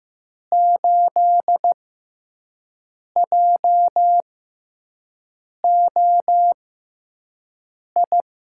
Morse code, heard as a keyed tone, about 15 wpm, spaced as for 5 wpm, 700 Hz. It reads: 8JOI
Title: USA Call Signs - 6x Character Spacing - 15wpm